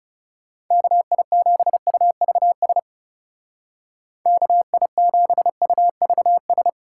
Morse code, heard as a keyed tone, about 36 wpm, 700 Hz. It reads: KI7UVS KI7UVS
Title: Speed Racing - USA 2x3 Call Signs 30wpm